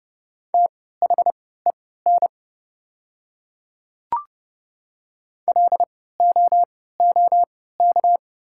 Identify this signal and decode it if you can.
Morse code, heard as a keyed tone, about 30 wpm, spaced as for 17 wpm, 700 Hz. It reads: THEN LOOK
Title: 3x Character Spacing - Top 100 Words - 30wpm